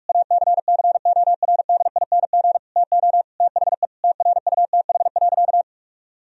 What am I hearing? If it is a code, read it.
Morse code, heard as a keyed tone, about 46 wpm, 700 Hz. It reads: ACCORDING TO THE TRUTH.
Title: Sentences from Top 1000 Words - Review Part 4 - 45wpm